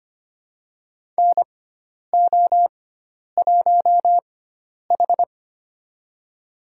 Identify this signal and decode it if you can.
Morse code, heard as a keyed tone, about 25 wpm, spaced as for 10 wpm, 700 Hz. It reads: NO1H
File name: Lesson 16g - Letter c - Call Signs 25wpm